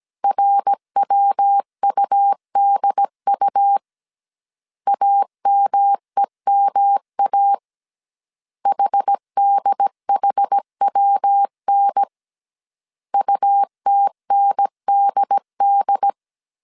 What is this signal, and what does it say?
Morse code, heard as a keyed tone, about 17 wpm, 800 Hz. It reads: RWUDU AMEMA HDHWN UTNDD